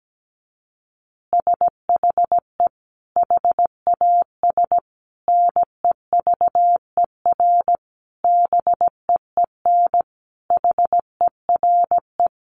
Morse code, heard as a keyed tone, about 17 wpm, 700 Hz. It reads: SHE HAS NEVER BEEN HERE
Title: Additional Easy Phrases 17wpm